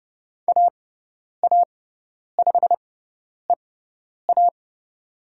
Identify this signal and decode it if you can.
Morse code, heard as a keyed tone, about 29 wpm, spaced as for 10 wpm, 700 Hz. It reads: AA5EA
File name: Lesson 17g - Letter u - Call Signs 30wpm